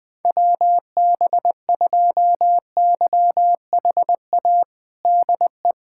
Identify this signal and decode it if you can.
Morse code, heard as a keyed tone, about 20 wpm, 700 Hz. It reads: WB2YHA DE
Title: USA Callsign DE Callsign - 6 Characters - 20wpm